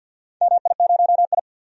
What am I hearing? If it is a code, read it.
Morse code, heard as a keyed tone, about 48 wpm, 700 Hz. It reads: MI0I
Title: DXCC-100 Call Signs - 4 Characters - 50wpm